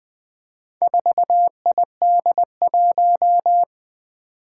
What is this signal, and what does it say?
Morse code, heard as a keyed tone, about 20 wpm, 700 Hz. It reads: 4ID1
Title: Four Challenging Letter-Number Combinations 20wpm